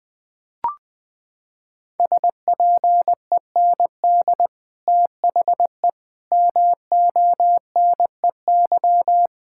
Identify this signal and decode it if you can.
Morse code, heard as a keyed tone, about 21 wpm, 700 Hz. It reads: SPEND THE MONEY
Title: Sentences from Top 1000 Words - Review Part 5 - 20wpm